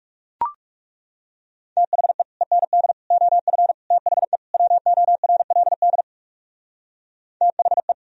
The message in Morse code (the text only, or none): THE END OF THE WORLD THE